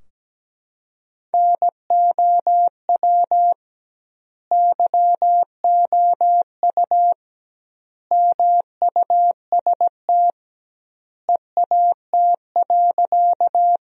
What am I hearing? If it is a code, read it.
Morse code, heard as a keyed tone, about 17 wpm, 700 Hz. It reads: NOW YOU MUST EAT.
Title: 2x Word-Spacing - Sentences from Top 300 Words - 17wpm